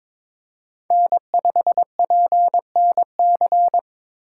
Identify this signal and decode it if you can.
Morse code, heard as a keyed tone, about 22 wpm, 700 Hz. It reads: N5PNC
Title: USA Call Signs - 5 Characters - 4x Character Spacing - 22wpm